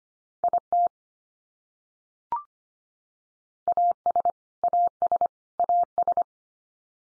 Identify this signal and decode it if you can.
Morse code, heard as a keyed tone, about 25 wpm, 700 Hz. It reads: IT AS AS AS
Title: Top 100 Words 25wpm x3